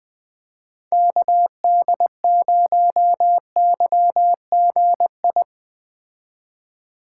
Morse code, heard as a keyed tone, about 21 wpm, 700 Hz. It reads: KD0YGI